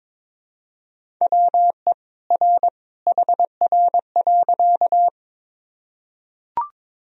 Morse code, heard as a keyed tone, about 22 wpm, 700 Hz. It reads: WE R HR.
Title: Sentences from Top 200 Words and Abbreviations 22wpm